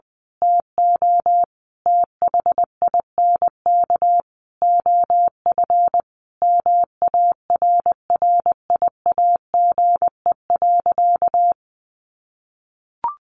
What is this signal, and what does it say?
Morse code, heard as a keyed tone, about 20 wpm, 700 Hz. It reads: TO THINK OF MARRIAGE.
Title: Sentences from Top 800 Words 20wpm